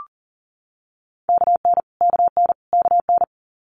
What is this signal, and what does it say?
Morse code, heard as a keyed tone, about 40 wpm, 700 Hz. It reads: KN KN KN
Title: Multi-letter Phonemes Repeated 3x 40wpm